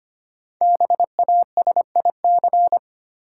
Morse code, heard as a keyed tone, about 25 wpm, 700 Hz. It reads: BASIC
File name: 5 Letter Words - First 3 Letters Given - 25wpm